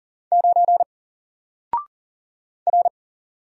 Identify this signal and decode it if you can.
Morse code, heard as a keyed tone, about 40 wpm, 700 Hz. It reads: 9 R
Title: Single Letter-Number Alt 40wpm